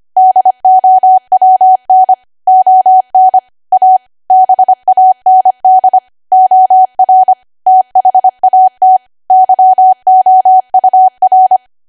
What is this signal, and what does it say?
Morse code, heard as a keyed tone, about 25 wpm, 750 Hz. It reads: DOWN ON A BAND OR THAT YOUR
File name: Archive / 25 WPM / 25WPM